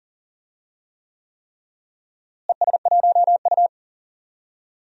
Morse code, heard as a keyed tone, about 40 wpm, 700 Hz. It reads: ES1U
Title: DXCC-100 Call Signs - 4 Characters - Repeated 3x - 40wpm